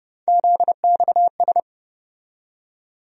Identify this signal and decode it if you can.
Morse code, heard as a keyed tone, about 30 wpm, 700 Hz. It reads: ZXS
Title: USA Call Sign Postfixes - 3 Letters 30wpm